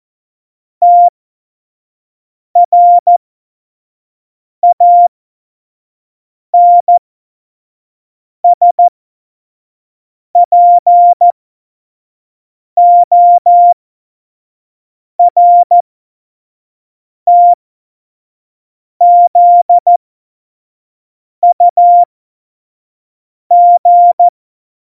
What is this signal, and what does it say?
Morse code, heard as a keyed tone, about 14 wpm, spaced as for 5 wpm, 700 Hz. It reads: TRANSPORTZUG
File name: Morsekurs / Witze / Witz 19. Februar 2019, 25 F70